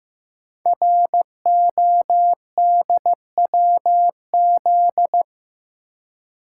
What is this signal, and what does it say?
Morse code, heard as a keyed tone, about 15 wpm, 700 Hz. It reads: RODWZ
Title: Code Groups - 5 Letters - 15wpm